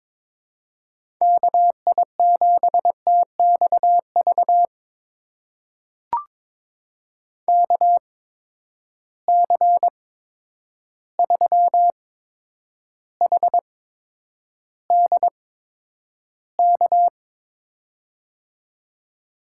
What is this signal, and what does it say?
Morse code, heard as a keyed tone, about 22 wpm, spaced as for 9 wpm, 700 Hz. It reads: KI7TXV KC3HDK
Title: USA Call Signs - 6 Characters - 8x Character Spacing - 22wpm